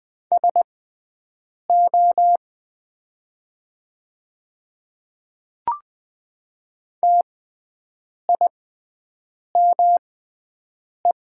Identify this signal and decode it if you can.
Morse code, heard as a keyed tone, about 20 wpm, spaced as for 7 wpm, 700 Hz. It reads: SO TIME
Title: 6x Character Spacing - Top 500 Words - 20wpm